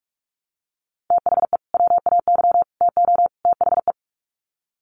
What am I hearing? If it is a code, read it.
Morse code, heard as a keyed tone, about 44 wpm, 700 Hz. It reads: THE WAY TO THE
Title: Sets of 4 Words 45wpm